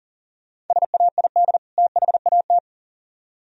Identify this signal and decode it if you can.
Morse code, heard as a keyed tone, about 48 wpm, 700 Hz. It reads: SAID THAT
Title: Speed-Racing - Sets of 2 Words 40wpm